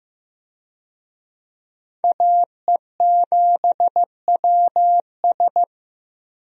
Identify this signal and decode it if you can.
Morse code, heard as a keyed tone, about 16 wpm, 700 Hz. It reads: AE7WS
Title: USA Call Signs - 5 Characters - Repeated 2x - 15wpm